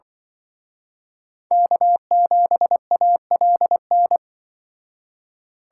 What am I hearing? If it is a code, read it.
Morse code, heard as a keyed tone, about 23 wpm, 700 Hz. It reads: K7ALN
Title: Speed Racing - USA 1x3 Call Signs 20wpm